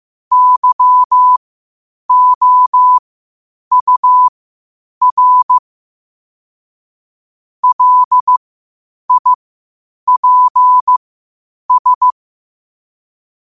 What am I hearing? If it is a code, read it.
Morse code, heard as a keyed tone, about 15 wpm, spaced as for 8 wpm, 1000 Hz. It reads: YOUR LIPS
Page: New quotes every day in morse code at 8 Words per minute.